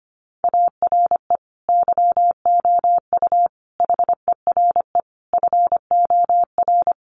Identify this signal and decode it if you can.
Morse code, heard as a keyed tone, about 25 wpm, 700 Hz. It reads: ARE YOU HERE FOR
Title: Sentences from Top 200 Words 25wpm